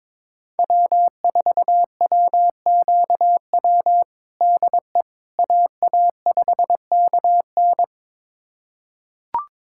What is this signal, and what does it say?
Morse code, heard as a keyed tone, about 23 wpm, 700 Hz. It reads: W4WQW DE AA5KN
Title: USA Callsign DE Callsign - 5 Characters - 22wpm